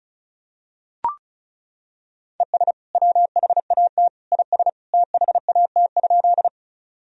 Morse code, heard as a keyed tone, about 34 wpm, 700 Hz. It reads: ES WHAT IS THAT?
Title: Sentences from Top 100 Words and Abbreviations 35wpm